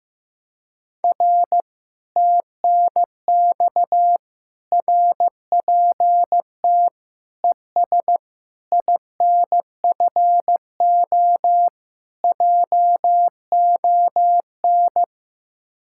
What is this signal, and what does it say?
Morse code, heard as a keyed tone, about 16 wpm, 700 Hz. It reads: R TNX RPT ES INFO JON
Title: Speed-Racing Common QSO Phrases Encore - 15wpm